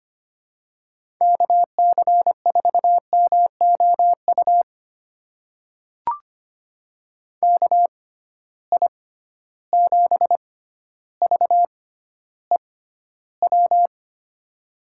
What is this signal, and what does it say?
Morse code, heard as a keyed tone, about 25 wpm, spaced as for 13 wpm, 700 Hz. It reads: KC4MOU KI7VEW
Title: USA Call Signs - 6 Characters - 6x Character Spacing - 25wpm